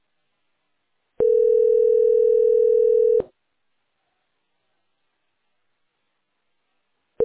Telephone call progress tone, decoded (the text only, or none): ringback tone